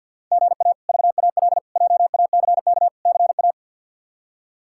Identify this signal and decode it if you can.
Morse code, heard as a keyed tone, about 53 wpm, 700 Hz. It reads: GA VAL JACK CA
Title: K1USN SST QSO Practice - 50wpm